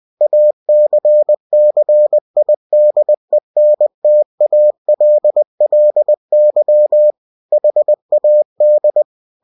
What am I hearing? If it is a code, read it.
Morse code, heard as a keyed tone, about 20 wpm, 600 Hz. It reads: ACCIDENTALLY HAD
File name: cw-books / winnie-the-pooh / winnie-20wpm-page-0072